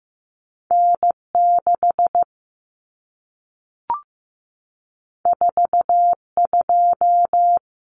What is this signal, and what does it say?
Morse code, heard as a keyed tone, about 15 wpm, 700 Hz. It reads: N6 42
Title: Common Cut Numbers - 2-Digit 15wpm